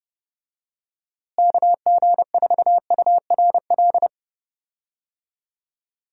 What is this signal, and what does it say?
Morse code, heard as a keyed tone, about 30 wpm, 700 Hz. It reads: KG4URL
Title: USA Call Signs - 6 Characters - Repeated 2x - 30wpm